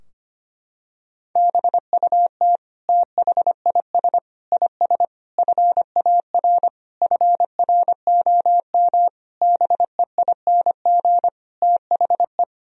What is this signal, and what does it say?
Morse code, heard as a keyed tone, about 25 wpm, 700 Hz. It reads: BUT THIS IS FAR FROM BEING THE
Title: Sentences from Top 1000 Words - Review Part 3 - 25wpm